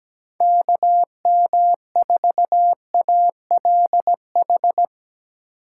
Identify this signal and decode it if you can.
Morse code, heard as a keyed tone, about 17 wpm, 700 Hz. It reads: KM4ALH
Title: USA Call Signs - 6 Characters - 8x Character Spacing - 17wpm